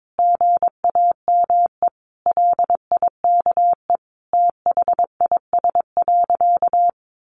Morse code, heard as a keyed tone, about 22 wpm, 700 Hz. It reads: GAME LIKE THIS.
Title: A Sound Of Thunder 22wpm